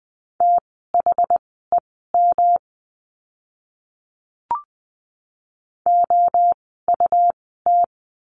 Morse code, heard as a keyed tone, about 20 wpm, spaced as for 14 wpm, 700 Hz. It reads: THEM OUT
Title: 192 Lesson 31f - Letter v - Words 20wpm